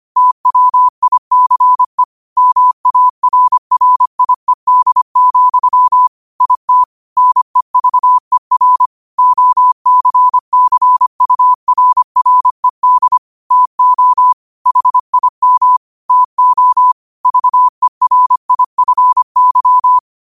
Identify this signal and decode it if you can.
Morse code, heard as a keyed tone, about 25 wpm, 1000 Hz. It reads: TWICE MARRIED, IT NEVER OCCURRED TO HIM TO VERIFY